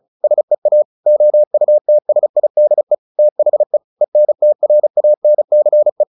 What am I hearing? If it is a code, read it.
Morse code, heard as a keyed tone, about 36 wpm, 600 Hz. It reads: SEA OUTSIDE THE ENTRANCE
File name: the old man and the sea 2 35wpm